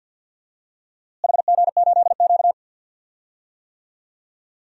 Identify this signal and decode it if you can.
Morse code, heard as a keyed tone, about 53 wpm, 700 Hz. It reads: HG8Q
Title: DXCC-100 Call Signs - 4 Characters - Repeated 2x - 50wpm